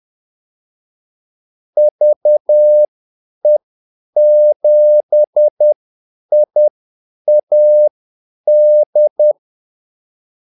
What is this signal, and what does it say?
Morse code, heard as a keyed tone, about 10 wpm, spaced as for 8 wpm, 600 Hz. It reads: VE7IAD
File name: Callsigns-10-8wpm